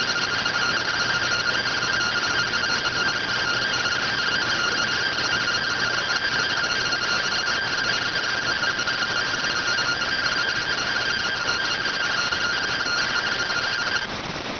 Two-tone teleprinter signal, about 170 Hz shift.